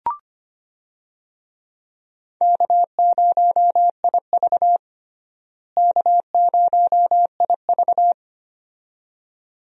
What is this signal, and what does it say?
Morse code, heard as a keyed tone, about 25 wpm, 700 Hz. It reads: K0IV K0IV
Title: USA Call Signs - 4 Characters - Repeated 2x - 25wpm